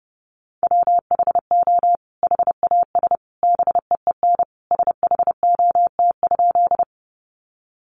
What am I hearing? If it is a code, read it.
Morse code, heard as a keyed tone, about 30 wpm, 700 Hz. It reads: WHO HAS BEEN SHOT?
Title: Sentences from Top 1100 Words 30wpm